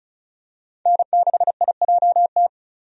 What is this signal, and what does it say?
Morse code, heard as a keyed tone, about 34 wpm, 700 Hz. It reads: N6IJT
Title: USA Call Signs - 5 Characters - 35wpm